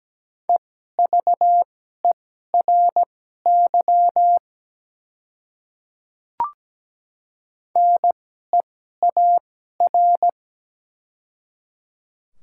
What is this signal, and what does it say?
Morse code, heard as a keyed tone, about 17 wpm, spaced as for 12 wpm, 700 Hz. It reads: EVERY NEAR
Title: 2x Character Spacing - Top 300 Words - 17wpm